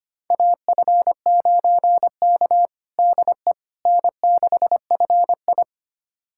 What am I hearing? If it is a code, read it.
Morse code, heard as a keyed tone, about 25 wpm, 700 Hz. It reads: AF9K DE N6FI
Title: USA Callsign DE Callsign - 4 Characters - 25wpm